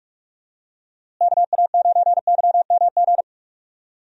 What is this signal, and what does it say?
Morse code, heard as a keyed tone, about 46 wpm, 700 Hz. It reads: KA9YMG
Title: USA Call Signs - 6 Characters - Repeated 3x - 45wpm